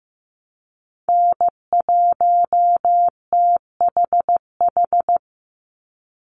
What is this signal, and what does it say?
Morse code, heard as a keyed tone, about 15 wpm, 700 Hz. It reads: N1THH
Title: USA Call Signs 15wpm